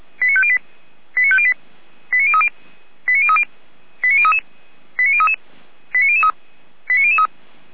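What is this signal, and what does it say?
Five-tone selective call (ZVEI), heard as quick runs of five tones, tones of about 70 ms. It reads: 89508 89508 89030 89030 89033 89033 89003 89003
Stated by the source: ZVEI five-tone call 89042 received on an SDR.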